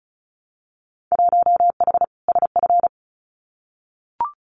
Morse code, heard as a keyed tone, about 36 wpm, 700 Hz. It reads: 1H SF
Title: Winter Field Day Exchanges 35wpm